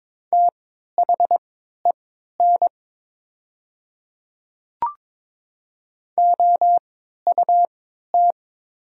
Morse code, heard as a keyed tone, about 22 wpm, spaced as for 12 wpm, 700 Hz. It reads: THEN OUT
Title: 3x Character Spacing - Top 100 Words - 22wpm